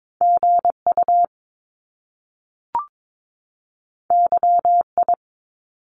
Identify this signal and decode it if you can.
Morse code, heard as a keyed tone, about 22 wpm, 700 Hz. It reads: GU YI
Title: Top Two Letter Combinations in Words 22wpm